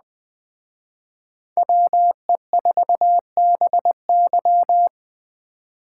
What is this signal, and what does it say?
Morse code, heard as a keyed tone, about 20 wpm, 700 Hz. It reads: WE4BY